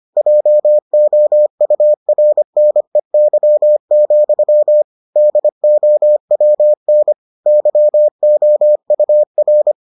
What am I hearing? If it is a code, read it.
Morse code, heard as a keyed tone, about 25 wpm, 600 Hz. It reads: JOURNEY, DOWN YOUR